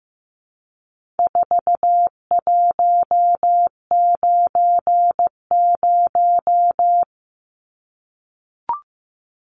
Morse code, heard as a keyed tone, about 15 wpm, 700 Hz. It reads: 4190
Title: Numbers - 4-Digit 15wpm